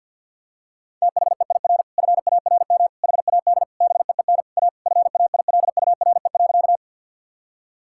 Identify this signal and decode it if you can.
Morse code, heard as a keyed tone, about 49 wpm, 700 Hz. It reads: THEIR FARM HAD BEEN A FAILURE.
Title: Sentences from Top 1000 Words - Review Part 6 - 50wpm